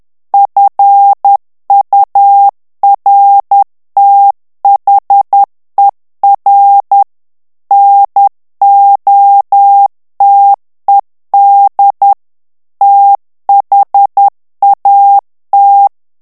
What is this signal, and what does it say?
Morse code, heard as a keyed tone, about 10 wpm, 800 Hz. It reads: FURTHER NOTED THAT
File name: ANGLAIS texte clair 3 vitesse 10 mots minute